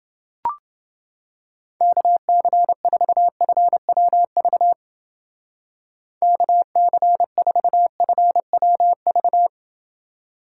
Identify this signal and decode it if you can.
Morse code, heard as a keyed tone, about 29 wpm, 700 Hz. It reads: KC4FWV KC4FWV